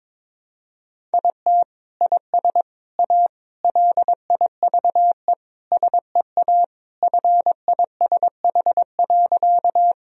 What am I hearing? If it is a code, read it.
Morse code, heard as a keyed tone, about 22 wpm, 700 Hz. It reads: IT IS A LIVE SEA FISH.
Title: Sentences from Top 1000 Words - Review Part 1 - 22wpm